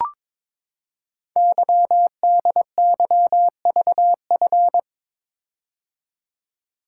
Morse code, heard as a keyed tone, about 22 wpm, 690 Hz. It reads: YDYVF
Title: Code Groups - 5 Letters - 22wpm